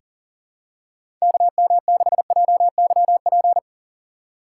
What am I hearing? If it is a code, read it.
Morse code, heard as a keyed tone, about 41 wpm, 700 Hz. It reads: KM6JYP
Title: USA Call Signs - 6 Characters - 4x Character Spacing - 40wpm